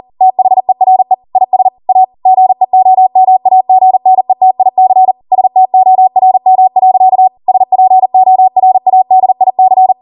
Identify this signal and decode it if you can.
Morse code, heard as a keyed tone, about 40 wpm, 750 Hz. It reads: THERE IS A GEOMAGNETIC STORM. SPORADIC